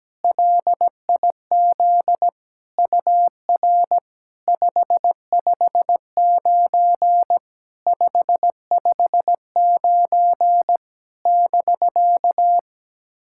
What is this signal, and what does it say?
Morse code, heard as a keyed tone, about 17 wpm, 700 Hz. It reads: LIZ UR 559 559 <BK>
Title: SOTA QSO Practice 17wpm